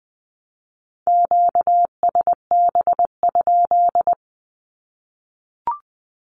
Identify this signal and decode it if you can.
Morse code, heard as a keyed tone, about 20 wpm, 700 Hz. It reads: QSB?
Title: Q-Codes 20wpm